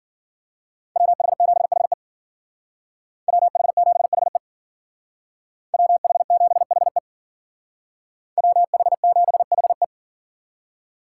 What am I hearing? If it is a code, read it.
Morse code, heard as a keyed tone, about 52 wpm, 700 Hz. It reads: WH7HE WH7HE WH7HE WH7HE